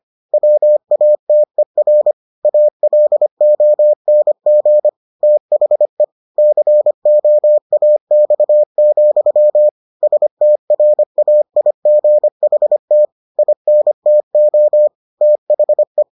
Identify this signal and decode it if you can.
Morse code, heard as a keyed tone, about 25 wpm, 600 Hz. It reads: WATER ALONG THE COAX, STRAIGHT INTO THE